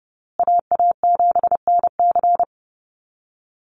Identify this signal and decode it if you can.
Morse code, heard as a keyed tone, about 30 wpm, 700 Hz. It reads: AA7NC
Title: USA Call Signs Repeated 3x 30wpm